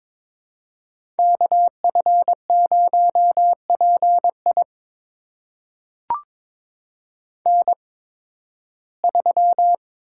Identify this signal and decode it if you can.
Morse code, heard as a keyed tone, about 22 wpm, spaced as for 14 wpm, 700 Hz. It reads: KF0PI N3
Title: USA Call Signs - 5 Characters - 8x Character Spacing - 22wpm